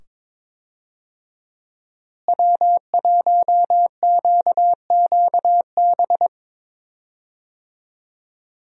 Morse code, heard as a keyed tone, about 22 wpm, 700 Hz. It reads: W1QQB
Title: USA Call Signs - 5 Characters - Repeated 2x - 22wpm